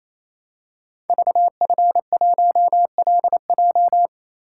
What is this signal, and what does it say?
Morse code, heard as a keyed tone, about 28 wpm, 700 Hz. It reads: VF1LJ
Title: Code Groups - 5 Letter:Numbers - 28wpm